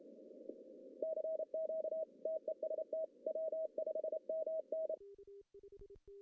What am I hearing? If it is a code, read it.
Morse code, heard as a keyed tone, about 28 wpm, 630 Hz. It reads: CQ TEST W5MN
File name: CW / cqww-cw-k5tr